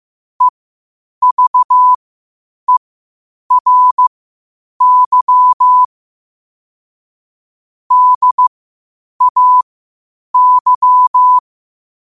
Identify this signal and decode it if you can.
Morse code, heard as a keyed tone, about 15 wpm, spaced as for 8 wpm, 1000 Hz. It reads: EVERY DAY
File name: qotdfiles / 8wpm